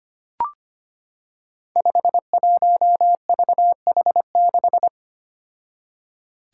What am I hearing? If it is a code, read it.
Morse code, heard as a keyed tone, about 25 wpm, 700 Hz. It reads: 51VH6
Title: Five Challenging Letter-Number Combinations 25wpm